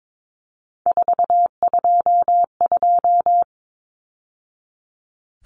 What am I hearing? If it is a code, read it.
Morse code, heard as a keyed tone, about 22 wpm, 700 Hz. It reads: 422
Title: Numbers - 3-Digit 22wpm